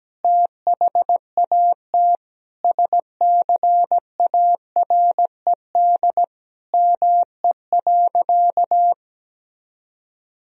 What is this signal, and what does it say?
Morse code, heard as a keyed tone, about 17 wpm, 700 Hz. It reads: THAT SCARED ME.
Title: Sentences from Top 1000 Words - Review Part 4 - 17wpm